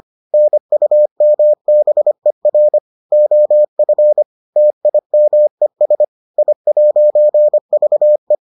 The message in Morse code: NUMBER OF TIMES I'VE

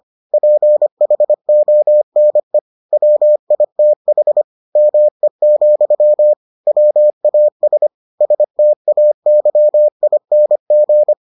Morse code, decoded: PHONE WITH ME, WAS STAYING

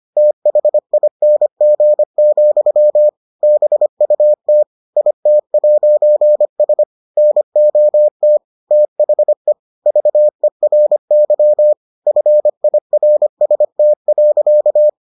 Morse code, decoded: THING, BUT IT'S NOT THE VERY FIRST.

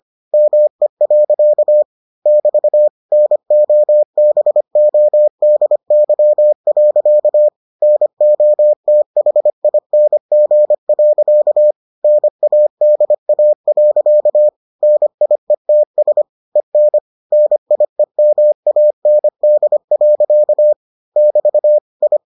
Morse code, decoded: ME. <BT> NOBODY. NOTHING. NADA. NIETS EN NIEMAND. <BT> I

